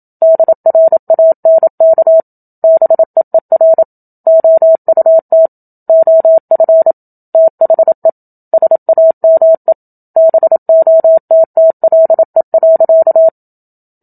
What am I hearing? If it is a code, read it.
Morse code, heard as a keyed tone, about 27 wpm, 650 Hz. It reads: DRANK BEER OUT OF THE SAME BOTTLE.
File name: Book Two / 01-Chapter 1 - 27 WPM